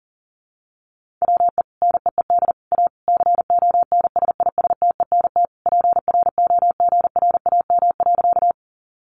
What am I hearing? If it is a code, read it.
Morse code, heard as a keyed tone, about 40 wpm, 700 Hz. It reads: WE NEED A CONSISTENT PROGRAM.
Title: Sentences from Top 1500 Words 40wpm